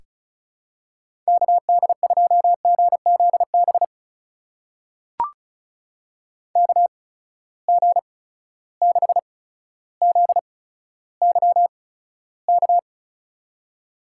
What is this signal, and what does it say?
Morse code, heard as a keyed tone, about 36 wpm, spaced as for 15 wpm, 700 Hz. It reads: KD2GZB KG6ZYK
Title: USA Call Signs - 6 Characters - 8x Character Spacing - 35wpm